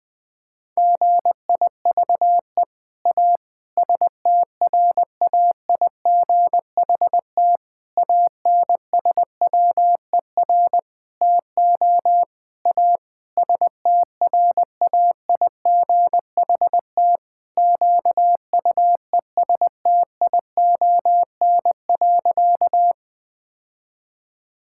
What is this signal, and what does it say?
Morse code, heard as a keyed tone, about 20 wpm, 700 Hz. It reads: GIVE A STRAIGHT ANSWER TO A STRAIGHT QUESTION.
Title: Sentences from Top 1000 Words - Review Part 6 - 20wpm